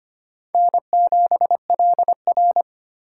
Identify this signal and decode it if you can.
Morse code, heard as a keyed tone, about 25 wpm, 700 Hz. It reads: N7LR
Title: K1USN SST QSO Practice - 25wpm